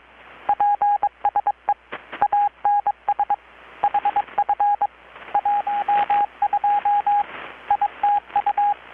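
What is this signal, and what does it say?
Morse code, heard as a keyed tone, about 21 wpm, 820 Hz, with radio noise. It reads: PSE ANS HF 12 ITU
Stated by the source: Morse from KPH's Night of Nights XX maritime radiotelegraphy event, received in Chiapas, Mexico about 3,800 km away on a Kenwood R-600 with a grounded 10 m long-wire antenna.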